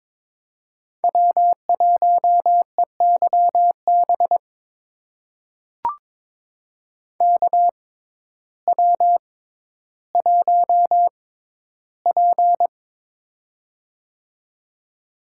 Morse code, heard as a keyed tone, about 23 wpm, spaced as for 12 wpm, 700 Hz. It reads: W1EYB KW1P
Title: USA Call Signs - 6x Character Spacing - 22wpm